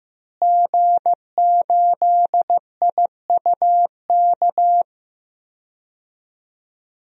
Morse code, heard as a keyed tone, about 16 wpm, 700 Hz. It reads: G8IUK
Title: DXCC-100 Call Signs - 5 Characters - Repeated 3x - 15wpm